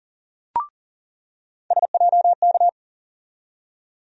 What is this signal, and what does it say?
Morse code, heard as a keyed tone, about 40 wpm, 700 Hz. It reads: SJK
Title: USA Call Sign Postfixes - 3 Letters 40wpm